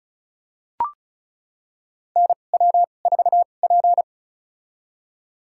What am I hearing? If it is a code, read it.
Morse code, heard as a keyed tone, about 34 wpm, spaced as for 25 wpm, 700 Hz. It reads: NW4P
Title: USA Call Signs - 4 Characters - 2x Character Spacing - 35wpm